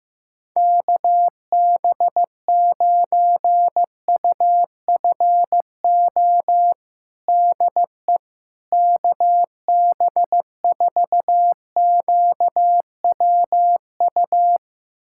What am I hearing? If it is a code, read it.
Morse code, heard as a keyed tone, about 15 wpm, 700 Hz. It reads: KB9UFO DE KB4QWU
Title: USA Callsign DE Callsign - 6 Characters - 15wpm